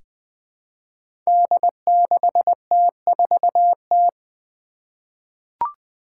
Morse code, heard as a keyed tone, about 20 wpm, 700 Hz. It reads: D6T4T
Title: Five Challenging Letter-Number Combinations 20wpm